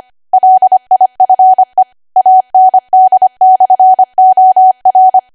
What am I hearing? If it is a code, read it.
Morse code, heard as a keyed tone, about 25 wpm, 750 Hz. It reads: LIFE AND/OR
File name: Archive / 25 WPM / 25WPM